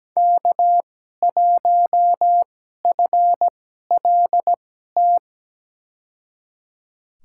Morse code, heard as a keyed tone, about 17 wpm, spaced as for 12 wpm, 700 Hz. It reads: K1FLT
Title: USA Call Signs - 5 Characters - 2x Character Spacing - 17wpm